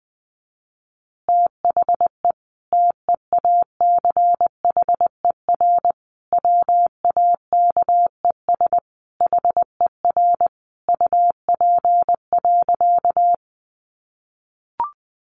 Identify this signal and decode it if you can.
Morse code, heard as a keyed tone, about 20 wpm, 700 Hz. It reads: THE TEACHER WAKES HER UP.
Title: Easy Phrases 20wpm